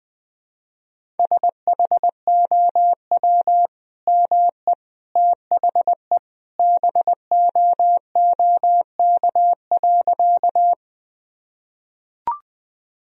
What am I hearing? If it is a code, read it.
Morse code, heard as a keyed tone, about 20 wpm, 700 Hz. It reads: SHOW ME THE BOOK.
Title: Sentences from Top 1000 Words - Review Part 1 - 20wpm